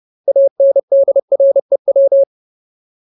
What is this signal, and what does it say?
Morse code, heard as a keyed tone, about 30 wpm, 550 Hz. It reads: ANDREW
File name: Ultimate Word List / 30WPM Ultimate